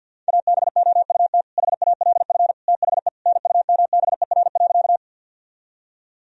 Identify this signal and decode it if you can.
Morse code, heard as a keyed tone, about 50 wpm, 700 Hz. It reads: ABOUT HALF THE NUMBER.
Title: Sentences from Top 1000 Words - Review Part 1 - 50wpm